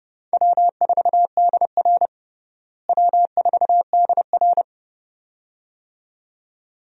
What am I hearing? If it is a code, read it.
Morse code, heard as a keyed tone, about 30 wpm, 700 Hz. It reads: W4DR W4DR
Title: USA Call Signs - 4 Characters - Repeated 2x - 30wpm